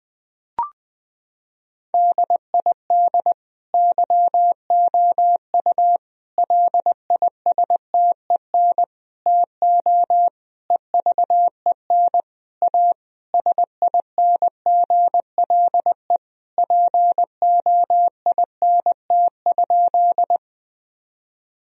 Morse code, heard as a keyed tone, about 20 wpm, 700 Hz. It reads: DID YOU LISTEN TO EVEN A SINGLE POINT?
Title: Sentences from Top 1000 Words - Review Part 4 - 20wpm